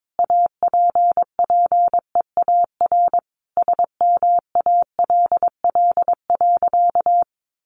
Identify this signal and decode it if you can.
Morse code, heard as a keyed tone, about 22 wpm, 700 Hz. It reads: APPEAR SMALL.
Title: Sentences from Top 1500 Words 22wpm